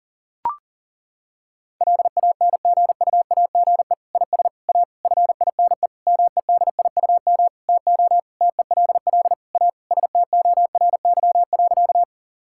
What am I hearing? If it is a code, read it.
Morse code, heard as a keyed tone, about 40 wpm, 700 Hz. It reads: LANGUAGE IS A FINE MEDIUM TO TELL A STORY.
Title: Sentences from Top 1000 Words - Review Part 6 - 40wpm